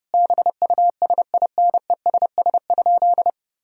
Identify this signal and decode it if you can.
Morse code, heard as a keyed tone, about 30 wpm, 700 Hz. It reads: BUSINESS?
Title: Sentences from Top 1000 Words - Review Part 4 - 30wpm